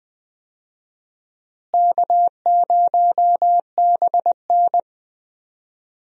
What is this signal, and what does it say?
Morse code, heard as a keyed tone, about 20 wpm, 700 Hz. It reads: K0BN
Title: USA Call Signs - 4 Characters - Repeated 3x - 20wpm